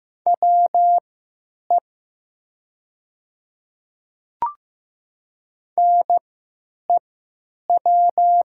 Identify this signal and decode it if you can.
Morse code, heard as a keyed tone, about 15 wpm, spaced as for 8 wpm, 700 Hz. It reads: WE NEW
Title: 3x Character Spacing - Top 200 Words - 15wpm